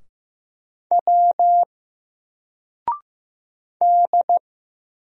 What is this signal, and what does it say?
Morse code, heard as a keyed tone, about 15 wpm, 700 Hz. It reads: W D
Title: Single Letter-Number Alt 15wpm